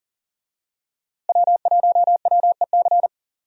Morse code, heard as a keyed tone, about 40 wpm, 700 Hz. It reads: W1WEC
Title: USA Call Signs - 5 Characters - Repeated 3x - 40wpm